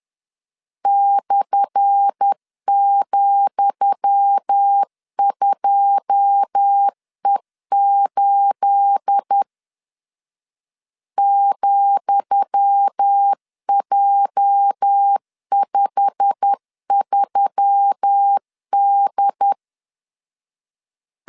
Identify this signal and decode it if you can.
Morse code, heard as a keyed tone, about 10 wpm, 800 Hz. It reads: /,2E8 ,J53D
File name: LEZIONI / 10 SIGNS / 10D COMA 10wpm 3013